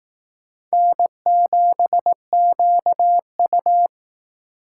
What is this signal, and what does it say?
Morse code, heard as a keyed tone, about 18 wpm, 700 Hz. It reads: N7QU
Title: Speed Racing - USA 1x2 Call Signs 15wpm